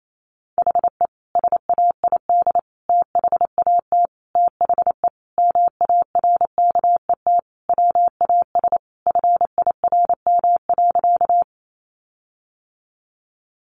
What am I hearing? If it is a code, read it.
Morse code, heard as a keyed tone, about 28 wpm, 700 Hz. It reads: HE SAID THAT THE MARKET WAS FIRM.
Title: Sentences from Top 1400 Words 28wpm